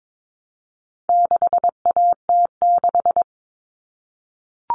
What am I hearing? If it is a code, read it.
Morse code, heard as a keyed tone, about 22 wpm, 700 Hz. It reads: 6AT6
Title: All Cut Numbers - 4-Digit 22wpm